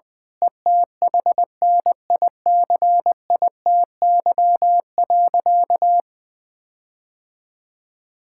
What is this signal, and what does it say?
Morse code, heard as a keyed tone, about 20 wpm, 700 Hz. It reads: ETHNICITY.
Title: Sentences with 9-Word Phrases - Spoken Beginning - 20wpm